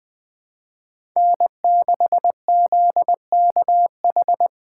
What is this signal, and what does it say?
Morse code, heard as a keyed tone, about 20 wpm, 700 Hz. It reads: N6ZKH
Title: USA Call Signs - 5 Characters - 2x Character Spacing - 20wpm